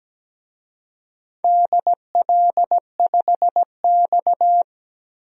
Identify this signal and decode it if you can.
Morse code, heard as a keyed tone, about 17 wpm, 700 Hz. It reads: DL5X
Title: DXCC-100 Call Signs - 4 Characters - 17wpm